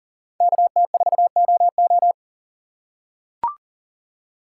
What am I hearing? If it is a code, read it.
Morse code, heard as a keyed tone, about 40 wpm, 700 Hz. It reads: KT4OO
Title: USA Call Signs - 2x Character Spacing - 40wpm